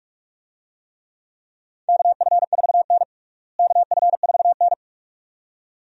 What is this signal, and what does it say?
Morse code, heard as a keyed tone, about 46 wpm, 700 Hz. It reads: KF4N KF4N
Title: USA Call Signs - 4 Characters - Repeated 2x - 45wpm